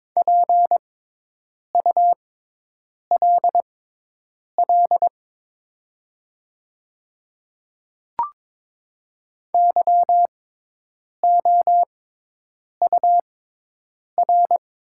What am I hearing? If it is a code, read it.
Morse code, heard as a keyed tone, about 22 wpm, spaced as for 8 wpm, 700 Hz. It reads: PULL YOUR
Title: 6x Character Spacing - Top 500 Words - 22wpm